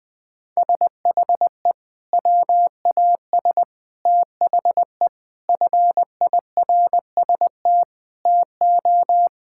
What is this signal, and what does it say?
Morse code, heard as a keyed tone, about 20 wpm, 700 Hz. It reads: SHE WAS THE FIRST TO
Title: Sentences with 3-Word Phrases - 20wpm